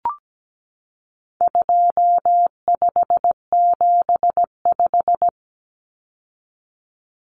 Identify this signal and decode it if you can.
Morse code, heard as a keyed tone, about 17 wpm, 700 Hz. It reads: 2575
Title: Numbers - 4-Digit 17wpm